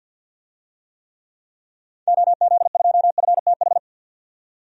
Morse code, heard as a keyed tone, about 51 wpm, 700 Hz. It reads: OZ2FTH
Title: DXCC-100 Call Signs - 6 Characters - 50wpm